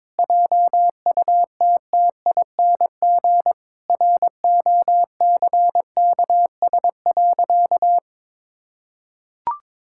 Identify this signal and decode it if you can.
Morse code, heard as a keyed tone, about 21 wpm, 700 Hz. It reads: JUTTING ROCKS.
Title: Pangram Challenge v2 - 22wpm